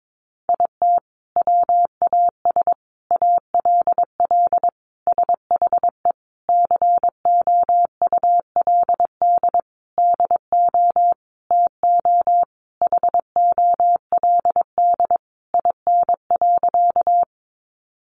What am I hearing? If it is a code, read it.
Morse code, heard as a keyed tone, about 22 wpm, 700 Hz. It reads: IT WAS ALL SHE COULD DO TO HOLD IN.
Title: Sentences from Top 400 Words 22wpm